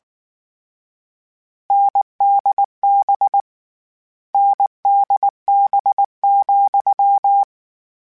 Morse code, heard as a keyed tone, about 19 wpm, 800 Hz. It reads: NDB NDB,
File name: Difficult Combos / Difficult Combos-mp3s / NDB NDB6 @18